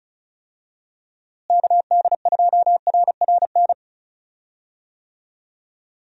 Morse code, heard as a keyed tone, about 36 wpm, 700 Hz. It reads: KD2RRN
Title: USA Call Signs - 6 Characters - Repeated 3x - 35wpm